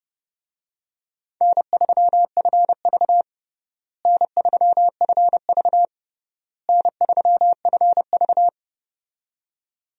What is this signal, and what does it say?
Morse code, heard as a keyed tone, about 31 wpm, 700 Hz. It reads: N3FV N3FV N3FV
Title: USA Call Signs - 4 Characters - Repeated 3x - 30wpm